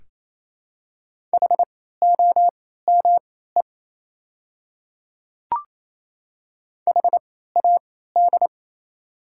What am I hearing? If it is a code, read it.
Morse code, heard as a keyed tone, about 28 wpm, spaced as for 16 wpm, 700 Hz. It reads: HOME HAD
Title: 3x Character Spacing - Top 500 Words - 28wpm